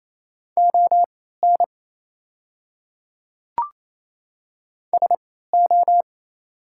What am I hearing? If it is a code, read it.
Morse code, heard as a keyed tone, about 27 wpm, spaced as for 16 wpm, 700 Hz. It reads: ON SO
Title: 3x Character Spacing - Top 300 Words - 28wpm